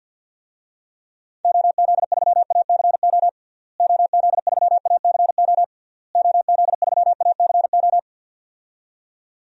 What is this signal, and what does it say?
Morse code, heard as a keyed tone, about 52 wpm, 700 Hz. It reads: OZ3ACO OZ3ACO OZ3ACO
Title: DXCC-100 Call Signs - 6 Characters - Repeated 3x - 50wpm